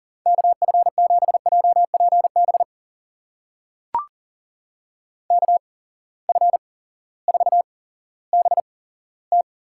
Morse code, heard as a keyed tone, about 40 wpm, spaced as for 18 wpm, 700 Hz. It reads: KF7JPB KF4BT
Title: USA Call Signs - 6 Characters - 8x Character Spacing - 40wpm